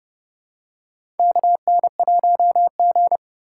Morse code, heard as a keyed tone, about 30 wpm, 700 Hz. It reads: KN1G
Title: USA Call Signs - 3x Character Spacing - 30wpm